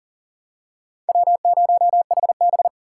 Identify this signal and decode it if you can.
Morse code, heard as a keyed tone, about 37 wpm, 690 Hz. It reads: W0HB
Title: USA Call Signs - 4 Characters - 2x Character Spacing - 40wpm